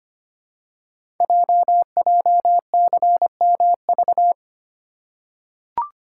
Morse code, heard as a keyed tone, about 25 wpm, 700 Hz. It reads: JJCMV
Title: Code Groups - 5 Letters - 25wpm